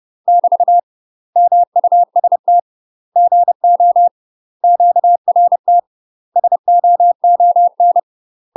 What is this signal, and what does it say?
Morse code, heard as a keyed tone, about 30 wpm, 700 Hz. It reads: <BT> MUST GO QRT SOON
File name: UK MASTER QSOs TRAINING / 30WPM / UK MASTER QSOs 0051